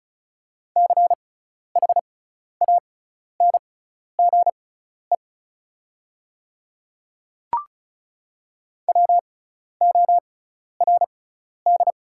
Morse code, heard as a keyed tone, about 36 wpm, spaced as for 12 wpm, 700 Hz. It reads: CHANGE WORD
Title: 6x Character Spacing - Top 300 Words - 35wpm